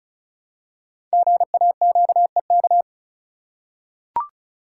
Morse code, heard as a keyed tone, about 36 wpm, 700 Hz. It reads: GAQEK
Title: Code Groups - 5 Letters - 35wpm